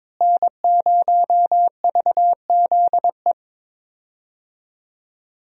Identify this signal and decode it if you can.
Morse code, heard as a keyed tone, about 22 wpm, 700 Hz. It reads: N0VZE
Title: USA Call Signs - 5 Characters - Repeated 3x - 22wpm